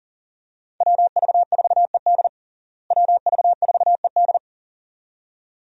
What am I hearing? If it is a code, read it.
Morse code, heard as a keyed tone, about 37 wpm, 700 Hz. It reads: WV4ED WV4ED
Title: USA Call Signs - 5 Characters - Repeated 2x - 40wpm